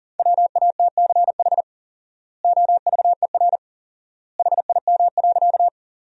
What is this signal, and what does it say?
Morse code, heard as a keyed tone, about 40 wpm, 700 Hz. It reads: WATCH OVER HIM.
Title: 4x Word-Spacing - Sentences from Top 300 Words - 40wpm